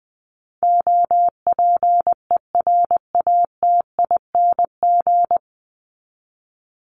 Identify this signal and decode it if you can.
Morse code, heard as a keyed tone, about 20 wpm, 700 Hz. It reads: OPERATING
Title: ING Suffix - Top 100 Words with Base Words 20wpm